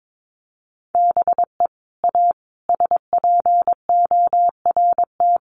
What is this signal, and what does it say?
Morse code, heard as a keyed tone, about 22 wpm, 700 Hz. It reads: BE A SPORT
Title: Sentences from Top 1500 Words 22wpm